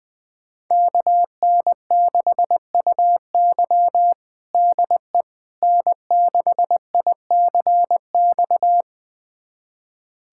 Morse code, heard as a keyed tone, about 21 wpm, 700 Hz. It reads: KN6UY DE N6ICX
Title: USA Callsign DE Callsign - 5 Characters - 20wpm